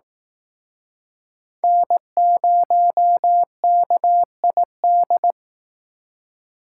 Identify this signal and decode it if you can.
Morse code, heard as a keyed tone, about 18 wpm, 700 Hz. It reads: N0KID
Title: Speed Racing - USA Call Signs 15wpm